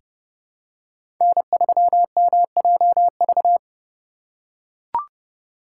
Morse code, heard as a keyed tone, about 30 wpm, 700 Hz. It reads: N3MJV
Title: USA Call Signs - 2x Character Spacing - 30wpm